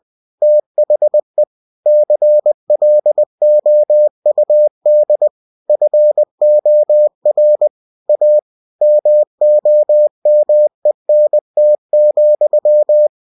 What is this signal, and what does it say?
Morse code, heard as a keyed tone, about 20 wpm, 600 Hz. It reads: THE CLOUD FOR A MOMENT,
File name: cw-books / winnie-the-pooh / winnie-20wpm-page-0009